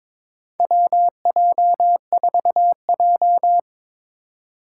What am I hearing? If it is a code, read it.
Morse code, heard as a keyed tone, about 22 wpm, 700 Hz. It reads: WJ4J
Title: USA Call Signs - 4 Characters - 6x Character Spacing - 22wpm